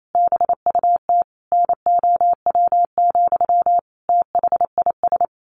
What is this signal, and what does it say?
Morse code, heard as a keyed tone, about 28 wpm, 700 Hz. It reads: BUT NOW, THIS
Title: A Sound Of Thunder 28wpm